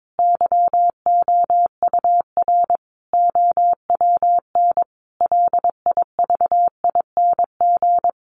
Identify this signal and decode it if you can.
Morse code, heard as a keyed tone, about 22 wpm, 700 Hz. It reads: YOUR OWN LIVING
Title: Sentences from Top 1500 Words 22wpm